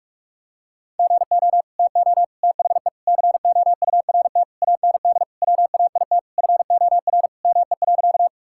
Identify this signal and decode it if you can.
Morse code, heard as a keyed tone, about 42 wpm, 700 Hz. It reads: GO TO THE COURT AND WAIT FOR ME.
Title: Sentences from Top 1600 Words 45wpm